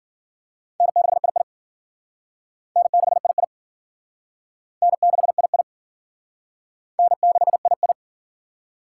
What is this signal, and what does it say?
Morse code, heard as a keyed tone, about 51 wpm, 700 Hz. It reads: N6II N6II N6II N6II